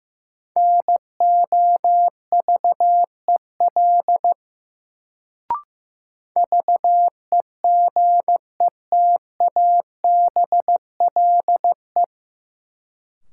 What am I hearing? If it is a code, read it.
Morse code, heard as a keyed tone, about 15 wpm, 700 Hz. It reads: NOVEL VEGETABLE
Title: Sentences from Top 1300 Words - New Word Drill Alt - 15wpm